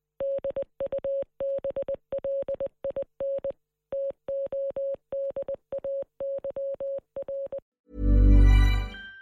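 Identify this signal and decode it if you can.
Morse code, heard as a keyed tone, about 20 wpm, 550 Hz. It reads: DUBLIN TODAY R
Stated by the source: Morse code message re-enacting the 1916 Rising broadcast by Marconi operator David Burke, transmitted on air by 37 radio stations across the state.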